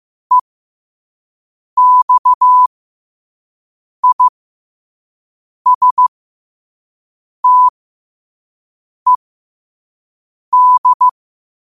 Morse code, heard as a keyed tone, about 15 wpm, spaced as for 5 wpm, 1000 Hz. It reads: EXISTED
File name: qotdfiles / 5wpm